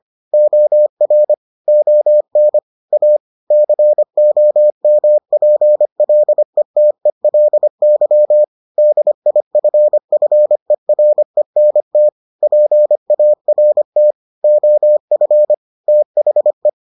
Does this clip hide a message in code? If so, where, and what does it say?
Yes: Morse code from 0.3 to 16.7 s, heard as a keyed tone, about 25 wpm, 600 Hz. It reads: OR ON A COMPLETELY DIFFERENT PART OF THE